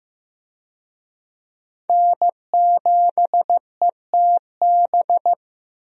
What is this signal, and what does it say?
Morse code, heard as a keyed tone, about 16 wpm, 700 Hz. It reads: N7ETB
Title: USA Call Signs - 5 Characters - Repeated 2x - 15wpm